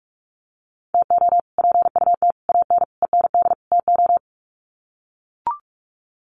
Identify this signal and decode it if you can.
Morse code, heard as a keyed tone, about 44 wpm, 690 Hz. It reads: TO PUT AN END TO
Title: Sets of 5 Words Encore2 45wpm